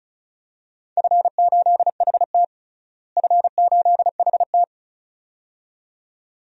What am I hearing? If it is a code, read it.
Morse code, heard as a keyed tone, about 34 wpm, 700 Hz. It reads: F8HT F8HT
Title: DXCC-100 Call Signs - 4 Characters - Repeated 2x - 35wpm